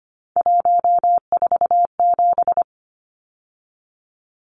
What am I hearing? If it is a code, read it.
Morse code, heard as a keyed tone, about 25 wpm, 700 Hz. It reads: 147
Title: Numbers - 3-Digit 25wpm